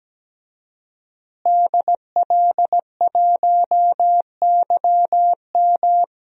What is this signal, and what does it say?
Morse code, heard as a keyed tone, about 17 wpm, 700 Hz. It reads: DL1YM
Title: DXCC-100 Call Signs - 5 Characters - Repeated 2x - 17wpm